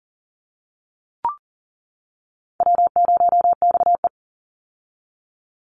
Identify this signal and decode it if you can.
Morse code, heard as a keyed tone, about 40 wpm, 700 Hz. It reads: W0XE
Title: Easy Call Signs - 40wpm